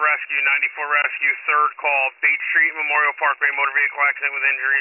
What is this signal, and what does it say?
Emergency Alert System (SAME) tones, heard as harsh buzzing bursts, with speech. It is heard from 0.9 to 4.2 s.